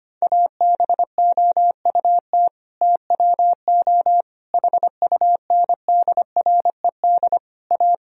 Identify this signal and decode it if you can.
Morse code, heard as a keyed tone, about 25 wpm, 700 Hz. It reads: ABOUT TWO HUNDRED A
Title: Sentences from Top 1000 Words - Review Part 2 - 25wpm